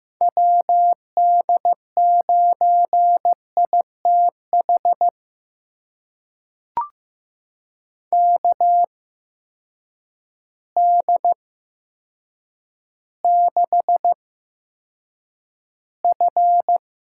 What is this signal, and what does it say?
Morse code, heard as a keyed tone, about 15 wpm, spaced as for 8 wpm, 700 Hz. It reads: WD9ITH KD6F
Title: USA Call Signs - 8x Character Spacing - 15wpm